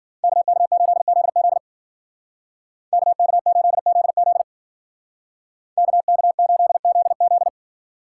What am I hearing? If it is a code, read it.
Morse code, heard as a keyed tone, about 48 wpm, 700 Hz. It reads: KK8ZZ KK8ZZ KK8ZZ
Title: Speed Racing - USA 2x2 Call Signs 40wpm